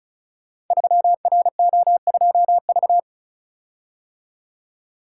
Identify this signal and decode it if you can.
Morse code, heard as a keyed tone, about 36 wpm, 700 Hz. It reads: 3RO2V
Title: Code Groups - 5 Letter:Numbers - 35wpm